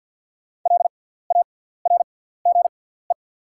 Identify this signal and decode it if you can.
Morse code, heard as a keyed tone, about 48 wpm, spaced as for 17 wpm, 700 Hz. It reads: LARGE
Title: 6x Character Spacing - Top 300 Words - 50wpm